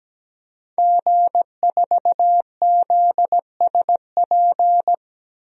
Morse code, heard as a keyed tone, about 17 wpm, 700 Hz. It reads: G4ZSP
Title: DXCC-100 Call Signs - 5 Characters - 17wpm